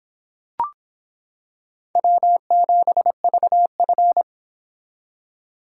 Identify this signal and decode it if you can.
Morse code, heard as a keyed tone, about 25 wpm, 700 Hz. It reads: W7VF